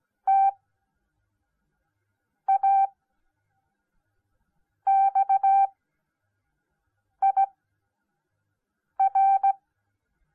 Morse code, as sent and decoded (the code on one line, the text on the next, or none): - .- -..- .. .-.
TAXIR